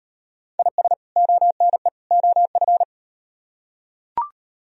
Morse code, heard as a keyed tone, about 38 wpm, 700 Hz. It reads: IS ONE OF